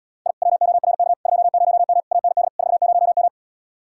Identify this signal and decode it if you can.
Morse code, heard as a keyed tone, about 77 wpm, 700 Hz. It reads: I WORK 20M AND 40M